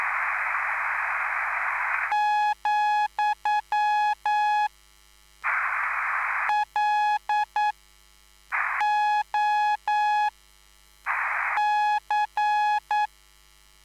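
Morse code, heard as a keyed tone, about 9 wpm, spaced as for 5 wpm, 840 Hz. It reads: , LOC
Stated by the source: Live CW QSO between PA2BX and SP3JFK on the 14055 kHz QRS frequency, at a basic 8 wpm.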